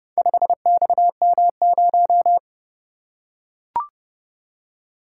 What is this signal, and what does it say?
Morse code, heard as a keyed tone, about 29 wpm, 700 Hz. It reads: 5XM0